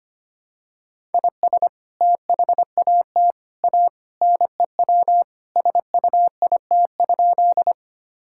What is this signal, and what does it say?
Morse code, heard as a keyed tone, about 25 wpm, 700 Hz. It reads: IS THAT A NEW SUIT?
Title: Sentences from Top 1600 Words 25wpm